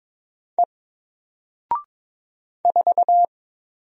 Morse code, heard as a keyed tone, about 22 wpm, 700 Hz. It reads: E 4
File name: Single-Letter-Number-22wpm